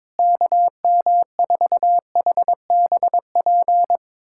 Morse code, heard as a keyed tone, about 21 wpm, 700 Hz. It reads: KM4HBP
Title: USA Callsign DE Callsign - 6 Characters - 22wpm